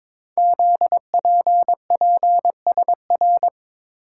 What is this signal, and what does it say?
Morse code, heard as a keyed tone, about 22 wpm, 700 Hz. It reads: ZPPSR
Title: Code Groups - 5 Letters - 22wpm